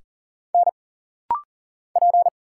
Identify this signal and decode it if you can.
Morse code, heard as a keyed tone, about 40 wpm, 700 Hz. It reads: N P
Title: Single Letters - Rapid-Fire 40wpm